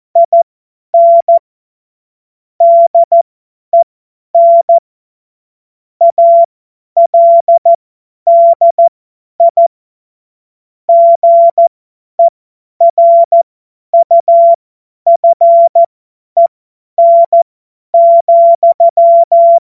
Morse code, heard as a keyed tone, about 14 wpm, spaced as for 10 wpm, 680 Hz. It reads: IN DEN ALDI GERUFEN,
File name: Morsekurs / Witze / Witz 29. Juni 2021, 50 F70